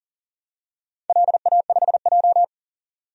Morse code, heard as a keyed tone, about 40 wpm, 700 Hz. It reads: LA5J
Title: DXCC-100 Call Signs - 4 Characters - 40wpm